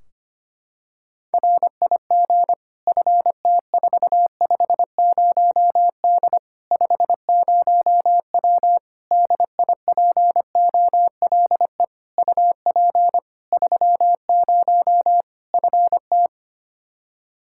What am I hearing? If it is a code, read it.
Morse code, heard as a keyed tone, about 25 wpm, 700 Hz. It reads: RIG FT450D 50W DIPOLE UP 30 FT
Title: Practice QSO - Exchange 2 - Info - 25wpm